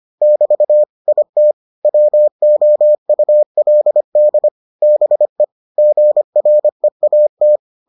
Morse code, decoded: <BT> IT WOULD BE GREAT

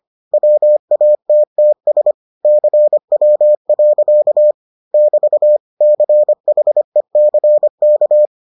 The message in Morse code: WATTS CW. <BT> CHECK

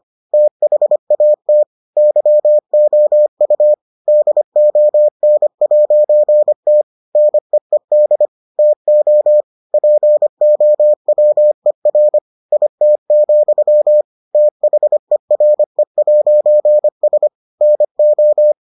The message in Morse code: THAT YOU DON'T NEED TO POWER IT, THERE'S NO

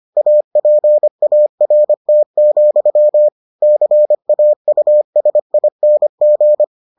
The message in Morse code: APART, CAUSING

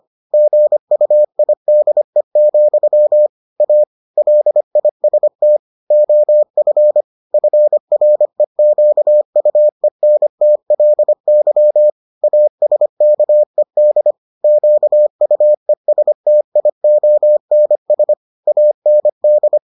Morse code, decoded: GUIDE, A LIST OF FREQUENTLY ASKED QUESTIONS AND

